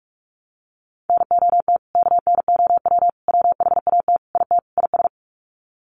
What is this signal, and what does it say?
Morse code, heard as a keyed tone, about 44 wpm, 700 Hz. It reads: NOT KNOW WHAT IT IS
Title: Sets of 5 Words Encore 45wpm